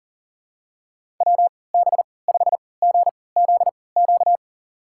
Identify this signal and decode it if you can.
Morse code, heard as a keyed tone, about 40 wpm, spaced as for 23 wpm, 700 Hz. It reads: WB5GZQ
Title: USA Call Signs - 6 Characters - 3x Character Spacing - 40wpm